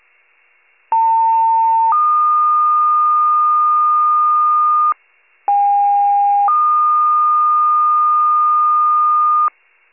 Two-tone page, heard as a one-second tone, then a three-second tone, about 900 then 1220 Hz, about 810 then 1220 Hz. It is heard from 0.9 to 9.5 s.